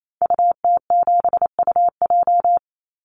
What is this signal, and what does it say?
Morse code, heard as a keyed tone, about 28 wpm, 700 Hz. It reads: UT7UJ
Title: Call Signs Repeated 2x - 28wpm